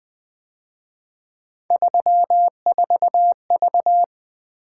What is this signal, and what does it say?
Morse code, heard as a keyed tone, about 23 wpm, 700 Hz. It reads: 34V
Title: Speed-Racing - Three Letter-Number Combinations - 20wpm